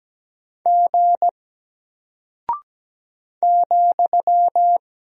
Morse code, heard as a keyed tone, about 17 wpm, 700 Hz. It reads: G ,
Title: Single Character (Letter:Number:Punctuation:Prosign) Alt 17wpm